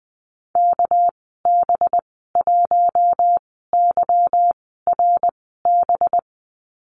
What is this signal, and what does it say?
Morse code, heard as a keyed tone, about 19 wpm, spaced as for 14 wpm, 700 Hz. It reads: KB1YRB
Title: 245 Lesson 39g - Letter z - Call Signs 20wpm